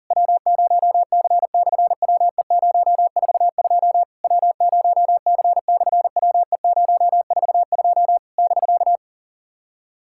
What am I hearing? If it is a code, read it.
Morse code, heard as a keyed tone, about 40 wpm, 690 Hz. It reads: W0C/WE042 W0C/WE042 <BK>
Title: SOTA QSO Practice 40wpm